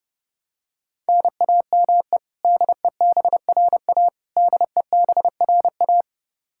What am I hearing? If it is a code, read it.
Morse code, heard as a keyed tone, about 30 wpm, 700 Hz. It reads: NAME DEBRA DEBRA
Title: Practice QSO - Exchange 1 - Beginning - 30wpm